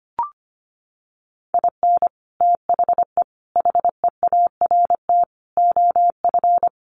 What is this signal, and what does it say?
Morse code, heard as a keyed tone, about 25 wpm, 700 Hz. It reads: IN THE HEART OF
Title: Sets of 5 Words 25wpm